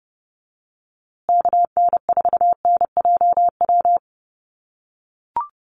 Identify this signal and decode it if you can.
Morse code, heard as a keyed tone, about 30 wpm, 700 Hz. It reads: KN4NJW
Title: USA Call Signs Repeated 2x 30wpm